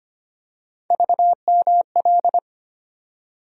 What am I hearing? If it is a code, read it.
Morse code, heard as a keyed tone, about 25 wpm, 700 Hz. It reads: VML
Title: USA Call Sign Postfixes - 3 Letters 25wpm